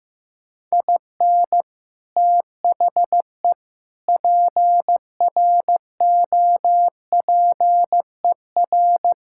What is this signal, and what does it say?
Morse code, heard as a keyed tone, about 16 wpm, 700 Hz. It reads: IN THE PROPER
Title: Sentences from Top 1000 Words - Review Part 5 - 15wpm